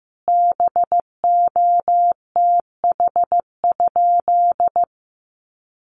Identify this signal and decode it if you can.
Morse code, heard as a keyed tone, about 15 wpm, 700 Hz. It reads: BOTH?
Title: Sentences from Top 1200 Words 15wpm